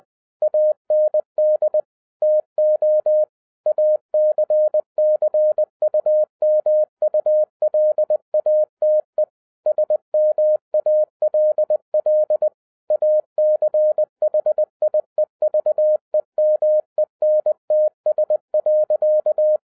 Morse code, heard as a keyed tone, about 20 wpm, 610 Hz. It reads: AND TO ACCUMULATE SMALL ACHIEVEMENTS.
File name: 2011-10-mp3 / n13-TIPS FROM MY 34 YEARS OF EXPERIENCE 20wpm